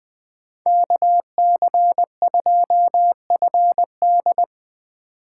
Morse code, heard as a keyed tone, about 20 wpm, 700 Hz. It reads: KC2FD